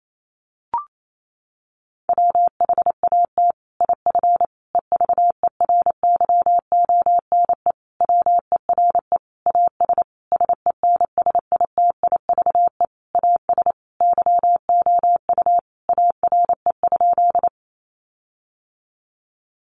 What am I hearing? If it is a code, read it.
Morse code, heard as a keyed tone, about 28 wpm, 700 Hz. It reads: WHAT IF EVERYONE WERE AS SENSITIVE AS YOU ARE?
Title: Sentences from Top 1400 Words 28wpm